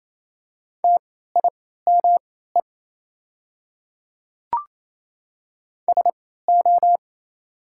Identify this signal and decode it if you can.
Morse code, heard as a keyed tone, about 28 wpm, spaced as for 16 wpm, 700 Hz. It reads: TIME SO
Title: 3x Character Spacing - Top 100 Words - 28wpm